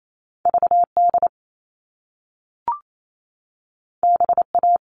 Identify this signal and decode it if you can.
Morse code, heard as a keyed tone, about 28 wpm, 700 Hz. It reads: VD BA
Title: All Cut Numbers - 2-Digit 28wpm